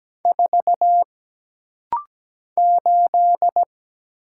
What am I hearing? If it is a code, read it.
Morse code, heard as a keyed tone, about 17 wpm, 700 Hz. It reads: 4 8
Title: Single Letter-Number - Rapid-Fire 17wpm